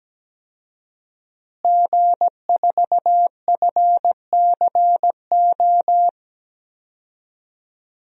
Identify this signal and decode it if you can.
Morse code, heard as a keyed tone, about 17 wpm, 700 Hz. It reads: G4FCO
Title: DXCC-100 Call Signs - 5 Characters - Repeated 3x - 17wpm